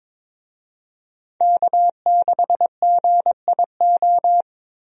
Morse code, heard as a keyed tone, about 22 wpm, 700 Hz. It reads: K6GIO
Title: USA Call Signs - 5 Characters - 22wpm